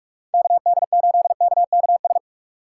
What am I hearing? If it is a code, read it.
Morse code, heard as a keyed tone, about 45 wpm, 700 Hz. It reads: KD8KKS
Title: USA Call Signs - 6 Characters - 2x Character Spacing - 45wpm